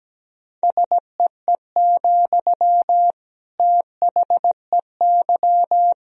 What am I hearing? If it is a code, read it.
Morse code, heard as a keyed tone, about 17 wpm, 700 Hz. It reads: SEE, THEY
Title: Sentences from Top 200 Words and Abbreviations 17wpm